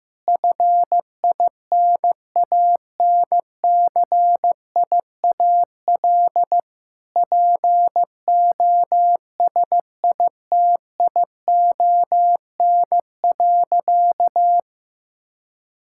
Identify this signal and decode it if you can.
Morse code, heard as a keyed tone, about 15 wpm, 700 Hz. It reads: FINANCIAL POSITION.
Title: Sentences from Top 1000 Words - Review Part 5 - 15wpm